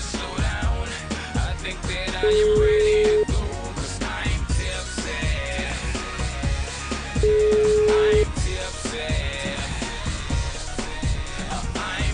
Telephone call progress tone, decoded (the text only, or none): ringback tone